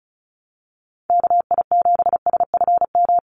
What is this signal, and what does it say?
Morse code, heard as a keyed tone, about 34 wpm, 700 Hz. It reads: KI7SFM
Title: USA Call Signs Repeated 3x 35wpm